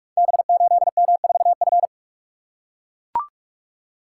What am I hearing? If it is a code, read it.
Morse code, heard as a keyed tone, about 44 wpm, 690 Hz. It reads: B8M4F
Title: Code Groups - 5 Letter:Numbers - 45wpm